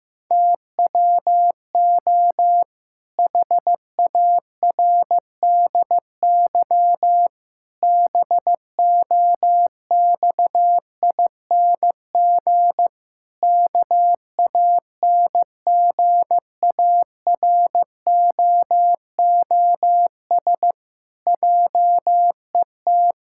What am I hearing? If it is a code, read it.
Morse code, heard as a keyed tone, about 15 wpm, 700 Hz. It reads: TWO HARDY BOXING KANGAROOS JET
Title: Pangram Challenge v2 - 15wpm